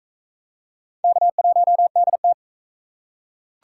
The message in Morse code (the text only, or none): K1DT